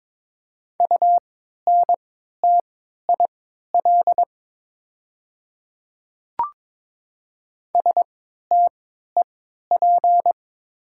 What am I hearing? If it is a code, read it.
Morse code, heard as a keyed tone, about 22 wpm, spaced as for 12 wpm, 700 Hz. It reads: UNTIL STEP
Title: 3x Character Spacing - Top 500 Words - 22wpm